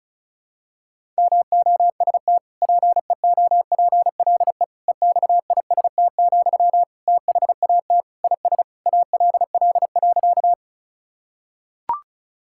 Morse code, heard as a keyed tone, about 36 wpm, 700 Hz. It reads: MOST PEOPLE EXIST, THAT IS ALL.
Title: Sentences from Top 1000 Words - Review Part 5 - 35wpm